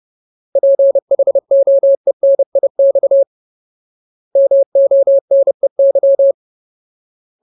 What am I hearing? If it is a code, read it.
Morse code, heard as a keyed tone, about 30 wpm, 550 Hz. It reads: PHOENIX MONEY
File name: Ultimate Word List / 30WPM Ultimate